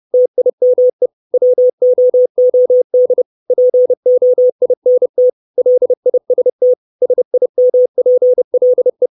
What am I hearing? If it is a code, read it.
Morse code, heard as a keyed tone, about 30 wpm, 500 Hz. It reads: TIME WOOD POINT LIST SIMPLE
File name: 632 Words 30WPM